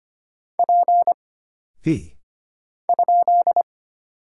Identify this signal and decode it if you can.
Morse code, heard as a keyed tone, about 25 wpm, 700 Hz. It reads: P ?
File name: Single Character (Letter:Number:Punctuation:Prosign) - Mind-Melt - 25wpm